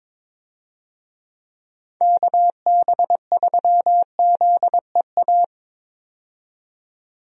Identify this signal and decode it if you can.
Morse code, heard as a keyed tone, about 22 wpm, 700 Hz. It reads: KB3ZEA
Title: USA Call Signs - 6 Characters - 22wpm